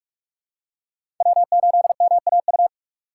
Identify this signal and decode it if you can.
Morse code, heard as a keyed tone, about 45 wpm, 700 Hz. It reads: W8MAU
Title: USA Call Signs - 5 Characters - 6x Character Spacing - 45wpm